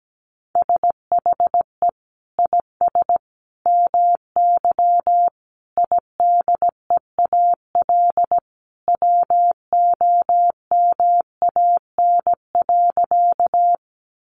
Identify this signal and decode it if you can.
Morse code, heard as a keyed tone, about 17 wpm, 700 Hz. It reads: SHE IS MY IDEAL WOMAN.
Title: Sentences from Top 1400 Words 17wpm